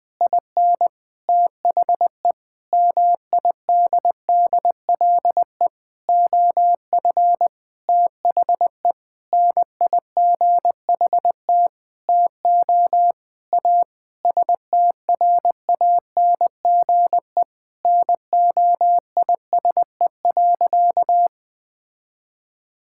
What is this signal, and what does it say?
Morse code, heard as a keyed tone, about 20 wpm, 700 Hz. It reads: IN THE MIDDLE OF THE NIGHT TO A STRANGE NOISE.
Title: Sentences with 8-Word Phrases - Spoken Beginning - 20wpm